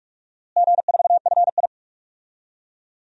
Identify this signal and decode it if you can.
Morse code, heard as a keyed tone, about 41 wpm, 700 Hz. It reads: G4FI
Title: DXCC-100 Call Signs - 4 Characters - Repeated 3x - 45wpm